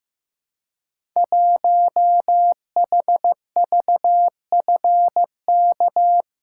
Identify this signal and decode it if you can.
Morse code, heard as a keyed tone, about 15 wpm, 700 Hz. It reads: 1HVFK
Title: Code Groups - 5 Letter:Numbers - 15wpm